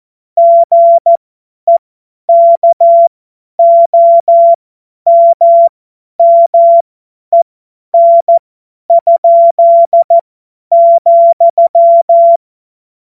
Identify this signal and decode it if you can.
Morse code, heard as a keyed tone, about 14 wpm, spaced as for 10 wpm, 680 Hz. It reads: GEKOMMEN?,
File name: Morsekurs / Witze / Witz 27. Januar 2026 50 F70